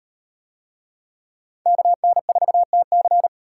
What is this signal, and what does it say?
Morse code, heard as a keyed tone, about 38 wpm, 700 Hz. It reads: KN4TC